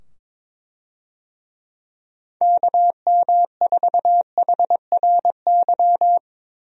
Morse code, heard as a keyed tone, about 22 wpm, 700 Hz. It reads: KM4HRY
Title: USA Call Signs - 6 Characters - 22wpm